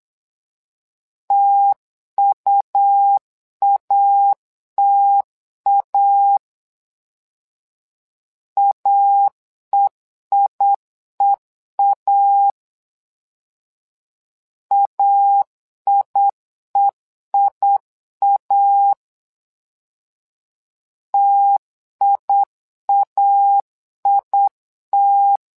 Morse code, heard as a keyed tone, about 8 wpm, 800 Hz. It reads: TUATA AEIEA AIEIA TIAIT
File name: LEZIONI / 2 UHDN / 2 U 8wpm 2028